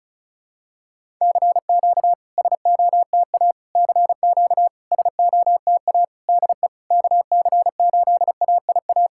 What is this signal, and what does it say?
Morse code, heard as a keyed tone, about 36 wpm, 700 Hz. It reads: CQ SOTA CQ SOTA DE KC8AIA
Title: SOTA QSO Practice 35wpm